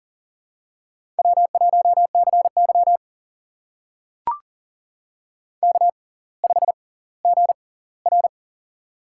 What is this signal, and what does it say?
Morse code, heard as a keyed tone, about 40 wpm, spaced as for 20 wpm, 700 Hz. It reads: W1CY K5GR
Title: USA Call Signs - 6x Character Spacing - 40wpm